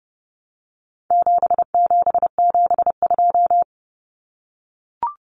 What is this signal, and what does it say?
Morse code, heard as a keyed tone, about 29 wpm, 700 Hz. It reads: 7772
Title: Numbers - 4-Digit 30wpm